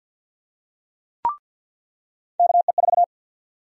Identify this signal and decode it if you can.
Morse code, heard as a keyed tone, about 50 wpm, 700 Hz. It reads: KE4
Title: USA Call Sign Prefixes - 50wpm